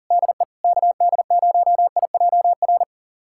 Morse code, heard as a keyed tone, about 40 wpm, 700 Hz. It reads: DE KD0IJR
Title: USA Callsign DE Callsign - 6 Characters - 40wpm